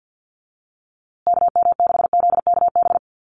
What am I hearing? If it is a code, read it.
Morse code, heard as a keyed tone, about 50 wpm, 700 Hz. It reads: KM6ZKB
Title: USA Call Signs Repeated 2x 50wpm